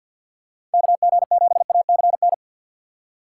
Morse code, heard as a keyed tone, about 48 wpm, 700 Hz. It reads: KG7ACN
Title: USA Call Signs - 6 Characters - 4x Character Spacing - 50wpm